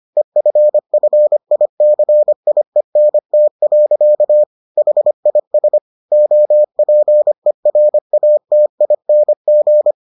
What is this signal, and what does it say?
Morse code, heard as a keyed tone, about 25 wpm, 600 Hz. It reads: EFFICIENT. HIS OPERATING